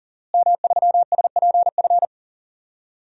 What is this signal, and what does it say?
Morse code, heard as a keyed tone, about 40 wpm, 700 Hz. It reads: M3SPF
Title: DXCC-100 Call Signs - 5 Characters - Repeated 3x - 40wpm